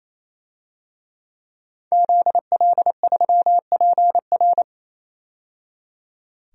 Morse code, heard as a keyed tone, about 28 wpm, 700 Hz. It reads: ZL3PR
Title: DXCC-100 Call Signs - 5 Characters - 28wpm